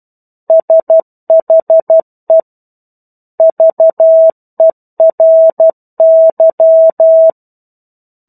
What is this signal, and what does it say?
Morse code, heard as a keyed tone, about 12 wpm, 650 Hz. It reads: SHE VERY
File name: 01-Chapter-01 12 WPM